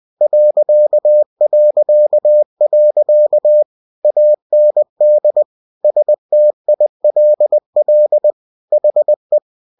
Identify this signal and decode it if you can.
Morse code, heard as a keyed tone, about 20 wpm, 600 Hz. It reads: ... AND STILL HE
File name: cw-books / winnie-the-pooh / winnie-20wpm-page-0091